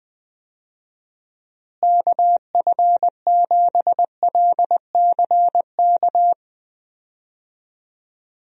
Morse code, heard as a keyed tone, about 20 wpm, 700 Hz. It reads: KF7LCK
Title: USA Call Signs - 6 Characters - Repeated 3x - 20wpm